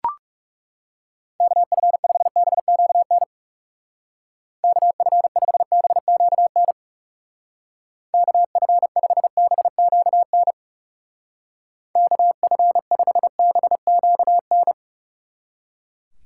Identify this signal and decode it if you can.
Morse code, heard as a keyed tone, about 38 wpm, 700 Hz. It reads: KF5BQN KF5BQN KF5BQN KF5BQN
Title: Speed Racing - USA 2x3 Call Signs 30wpm